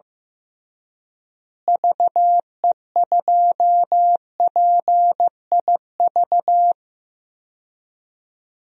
Morse code, heard as a keyed tone, about 15 wpm, 700 Hz. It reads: VE2PIV
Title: DXCC-100 Call Signs - 6 Characters - Repeated 3x - 15wpm